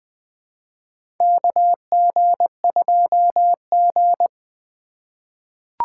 Morse code, heard as a keyed tone, about 20 wpm, 700 Hz. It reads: KG2G
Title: USA Call Signs - 4 Characters - 3x Character Spacing - 20wpm